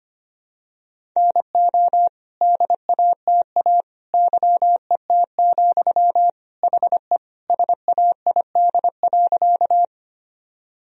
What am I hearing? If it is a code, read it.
Morse code, heard as a keyed tone, about 25 wpm, 700 Hz. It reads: NO DATA YET, HE SAID.
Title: Sentences from Top 1000 Words - Review Part 3 - 25wpm